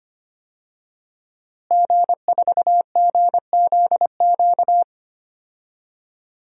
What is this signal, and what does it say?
Morse code, heard as a keyed tone, about 25 wpm, 700 Hz. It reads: G4GZQ
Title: DXCC-100 Call Signs - 5 Characters - Repeated 3x - 25wpm